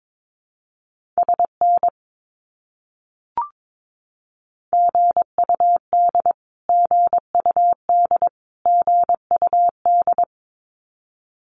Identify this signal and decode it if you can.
Morse code, heard as a keyed tone, about 22 wpm, 700 Hz. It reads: SN GUD GUD GUD
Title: Abbreviations Repeated 3x - 22wpm